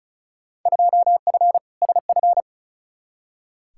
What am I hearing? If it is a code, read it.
Morse code, heard as a keyed tone, about 36 wpm, 700 Hz. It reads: 2F SF
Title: Field Day Exchanges - 35wpm